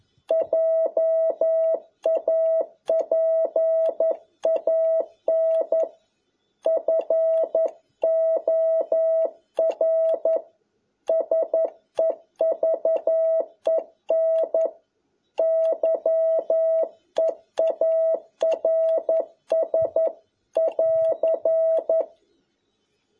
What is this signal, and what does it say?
Morse code, heard as a keyed tone, about 11 wpm, 650 Hz. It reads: JAPAN FOR SEVEN YEARS <AR>
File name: jh8cbh / 2021cw